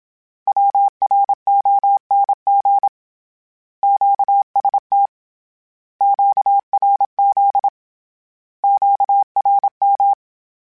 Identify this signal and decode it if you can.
Morse code, heard as a keyed tone, about 27 wpm, 800 Hz. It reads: WRONG QST QRZ QRM
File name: Misc / Misc-mp3s / Words @25